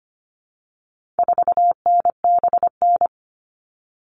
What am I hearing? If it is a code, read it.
Morse code, heard as a keyed tone, about 25 wpm, 700 Hz. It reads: 4NBN
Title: All Cut Numbers - 4-Digit 25wpm